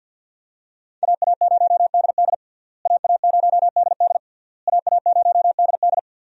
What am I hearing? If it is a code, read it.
Morse code, heard as a keyed tone, about 52 wpm, 700 Hz. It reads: AA0DD AA0DD AA0DD
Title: USA Call Signs - 5 Characters - Repeated 3x - 50wpm